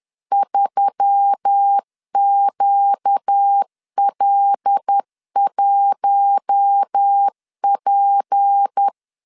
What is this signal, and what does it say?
Morse code, heard as a keyed tone, about 10 wpm, 800 Hz. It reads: 3QL1P
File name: LEZIONI / 8 YJ45 / 8 YJ 10wpm 2651